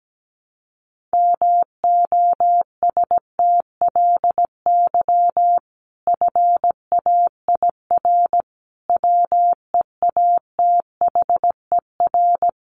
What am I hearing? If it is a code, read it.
Morse code, heard as a keyed tone, about 17 wpm, 700 Hz. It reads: MOSTLY FAIR WEATHER
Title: Sentences from Top 1300 Words 17wpm